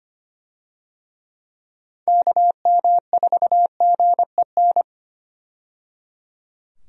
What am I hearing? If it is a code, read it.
Morse code, heard as a keyed tone, about 25 wpm, 700 Hz. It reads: KM4GEN
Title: USA Call Signs - 6 Characters - 25wpm